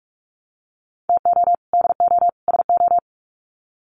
Morse code, heard as a keyed tone, about 44 wpm, 700 Hz. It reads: TO DO SO
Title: Sets of 3 Words 45wpm